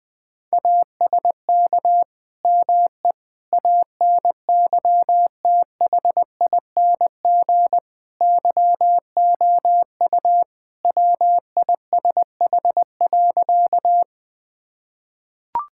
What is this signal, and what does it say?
Morse code, heard as a keyed tone, about 20 wpm, 700 Hz. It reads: ASK ME ANYTHING YOU WISH.
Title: Sentences from Top 1600 Words 20wpm